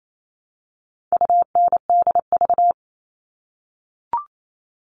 Morse code, heard as a keyed tone, about 28 wpm, 700 Hz. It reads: UNDV
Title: All Cut Numbers - 4-Digit 28wpm